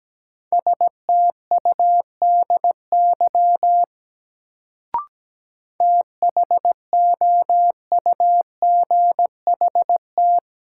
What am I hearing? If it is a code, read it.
Morse code, heard as a keyed tone, about 17 wpm, 700 Hz. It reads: STUDY THOUGHT
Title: Sentences from Top 300 Words - New Word Drill - 17wpm